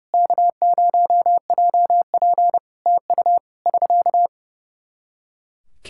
Morse code, heard as a keyed tone, about 30 wpm, 700 Hz. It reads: K0JP TU <SK>
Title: Practice QSO - Exchange 4 - Ending - 30wpm